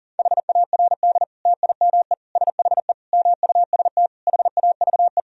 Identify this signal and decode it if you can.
Morse code, heard as a keyed tone, about 37 wpm, 700 Hz. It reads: HARD TIME SHE MUST HAVE
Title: Sentences from Top 1000 Words - Review Part 1 - 40wpm